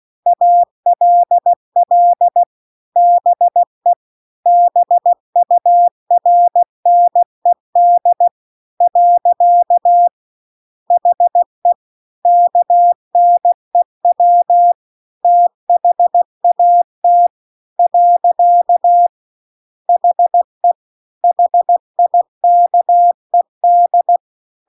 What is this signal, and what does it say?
Morse code, heard as a keyed tone, about 16 wpm, 700 Hz. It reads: ALL BE BURNED . HE KNEW THAT . HE HIKED